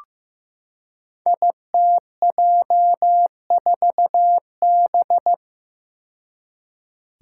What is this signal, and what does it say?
Morse code, heard as a keyed tone, about 15 wpm, 700 Hz. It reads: ITJ4B
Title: Five Challenging Letter-Number Combinations 15wpm